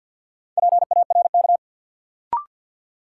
Morse code, heard as a keyed tone, about 50 wpm, 700 Hz. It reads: PARK
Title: Sentences from Top 1300 Words - New Word Drill Alt - 50wpm